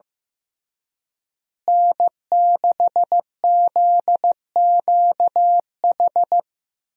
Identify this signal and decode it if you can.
Morse code, heard as a keyed tone, about 15 wpm, 700 Hz. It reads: N6ZQH
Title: USA Call Signs - 5 Characters - Repeated 2x - 15wpm